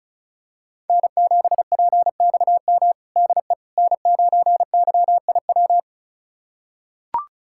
Morse code, heard as a keyed tone, about 36 wpm, 700 Hz. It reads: N7PXM DE N9YIW
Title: USA Callsign DE Callsign - 5 Characters - 35wpm